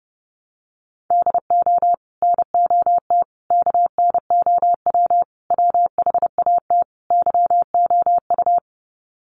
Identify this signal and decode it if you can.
Morse code, heard as a keyed tone, about 30 wpm, 700 Hz. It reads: DO NOT KNOW WHAT YOU
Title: Sets of 5 Words 30wpm